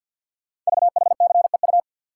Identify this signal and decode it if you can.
Morse code, heard as a keyed tone, about 51 wpm, 700 Hz. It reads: VHCEU
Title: Code Groups - 5 Letters - 50wpm